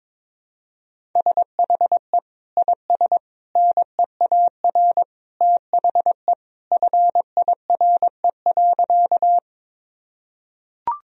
Morse code, heard as a keyed tone, about 22 wpm, 700 Hz. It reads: SHE IS NEAR THE FIRE.
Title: Sentences from Top 1000 Words - Review Part 1 - 22wpm